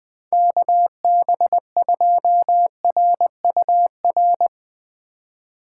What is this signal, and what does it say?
Morse code, heard as a keyed tone, about 20 wpm, 700 Hz. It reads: KB2RUR
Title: USA Call Signs - 6 Characters - 4x Character Spacing - 20wpm